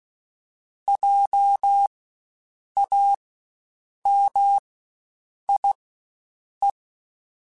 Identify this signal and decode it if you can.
Morse code, heard as a keyed tone, about 16 wpm, spaced as for 7 wpm, 800 Hz. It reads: JAMIE
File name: Spaced Alphas / Spaced Alphas mp3s / Js sp @15